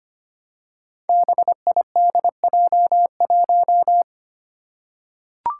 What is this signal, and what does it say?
Morse code, heard as a keyed tone, about 25 wpm, 700 Hz. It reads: BIDJ1
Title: Five Challenging Letter-Number Combinations 25wpm